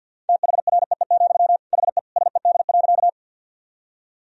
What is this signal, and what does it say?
Morse code, heard as a keyed tone, about 52 wpm, 700 Hz. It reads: THREE, HE SED.
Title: Sentences from Top 200 Words and Abbreviations 50wpm